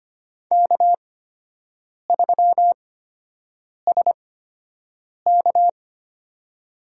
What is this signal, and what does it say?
Morse code, heard as a keyed tone, about 25 wpm, spaced as for 7 wpm, 700 Hz. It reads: K3SK
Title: USA Call Signs - 4 Characters - 8x Character Spacing - 25wpm